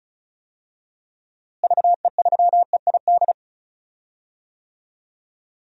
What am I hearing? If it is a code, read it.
Morse code, heard as a keyed tone, about 36 wpm, 700 Hz. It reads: VE3EID
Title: DXCC-100 Call Signs - 6 Characters - Repeated 3x - 35wpm